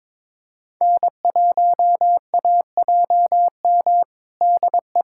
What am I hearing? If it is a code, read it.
Morse code, heard as a keyed tone, about 21 wpm, 700 Hz. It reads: N1AJM DE
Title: USA Callsign DE Callsign 22wpm